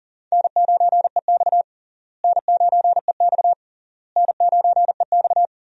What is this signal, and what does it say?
Morse code, heard as a keyed tone, about 40 wpm, 700 Hz. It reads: N9EX N9EX N9EX
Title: USA Call Signs - 4 Characters - Repeated 3x - 40wpm